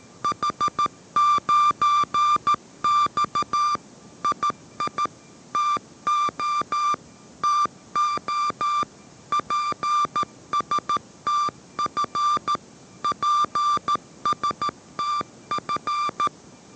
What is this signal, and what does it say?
Morse code, heard as a keyed tone, about 13 wpm, 1180 Hz. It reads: H9X II TO TO PSTF PSTF
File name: 14wpm